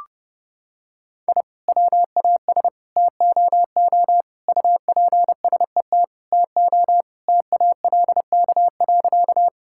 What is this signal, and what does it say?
Morse code, heard as a keyed tone, about 30 wpm, 700 Hz. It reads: I WAS TOO UPSET TO TALK.
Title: Sentences from Top 1000 Words - Review Part 5 - 30wpm